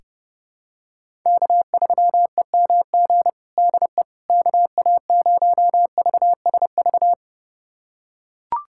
Morse code, heard as a keyed tone, about 30 wpm, 700 Hz. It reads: K3EMG DE KA0VSV
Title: USA Callsign DE Callsign 30wpm